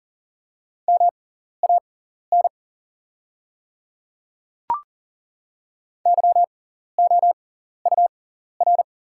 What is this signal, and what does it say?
Morse code, heard as a keyed tone, about 40 wpm, spaced as for 14 wpm, 690 Hz. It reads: MAN YOUR
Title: 6x Character Spacing - Top 200 Words - 40wpm